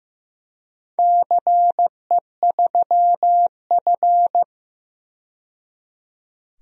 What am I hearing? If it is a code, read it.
Morse code, heard as a keyed tone, about 15 wpm, 700 Hz. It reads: CE3F
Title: DXCC-100 Call Signs - 4 Characters - 15wpm